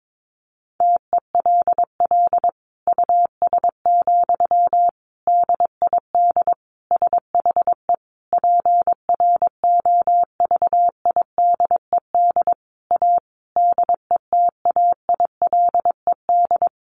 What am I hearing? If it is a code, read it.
Morse code, heard as a keyed tone, about 22 wpm, 700 Hz. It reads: TELL US, DID SHE PROVIDED A DETAILED
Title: Sentences from Top 1500 Words 22wpm